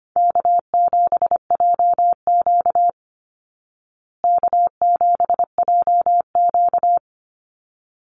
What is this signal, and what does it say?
Morse code, heard as a keyed tone, about 25 wpm, 700 Hz. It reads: K7JQ K7JQ
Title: Easy Call Signs Repeated 2x - 25wpm